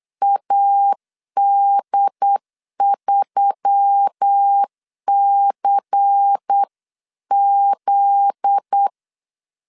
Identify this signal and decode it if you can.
Morse code, heard as a keyed tone, about 8 wpm, 800 Hz. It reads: AD3CZ
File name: LEZIONI / 8 YJ45 / 8 YJ4 8wpm 3246